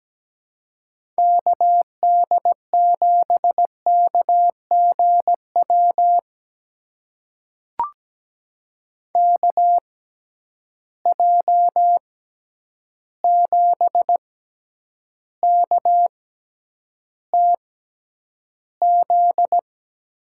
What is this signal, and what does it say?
Morse code, heard as a keyed tone, about 17 wpm, spaced as for 9 wpm, 700 Hz. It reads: KD7KGW KJ7KTZ
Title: USA Call Signs - 6 Characters - 6x Character Spacing - 17wpm